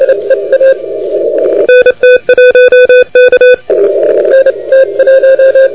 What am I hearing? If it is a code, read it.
Morse code, heard as a keyed tone, about 27 wpm, 2500 Hz, with radio noise. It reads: NT1K
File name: recordings / YO2LEA de NT1K ARRL10M CW 28MHz